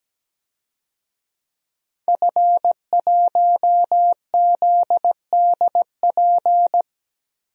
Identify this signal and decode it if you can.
Morse code, heard as a keyed tone, about 17 wpm, 700 Hz. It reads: F1ZDP
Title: DXCC-100 Call Signs - 5 Characters - Repeated 3x - 17wpm